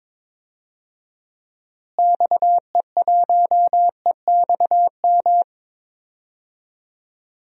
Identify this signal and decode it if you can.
Morse code, heard as a keyed tone, about 22 wpm, 700 Hz. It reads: XE1EXM
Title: DXCC-100 Call Signs - 6 Characters - Repeated 3x - 22wpm